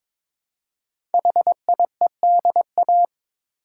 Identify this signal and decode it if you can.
Morse code, heard as a keyed tone, about 22 wpm, 700 Hz. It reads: HIEDA
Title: Code Groups - 5 Letters - 22wpm